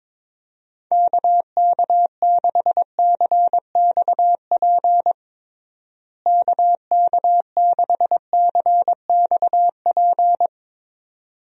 Morse code, heard as a keyed tone, about 22 wpm, 700 Hz. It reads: KK6CXP KK6CXP
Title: USA Call Signs - 6 Characters - Repeated 2x - 22wpm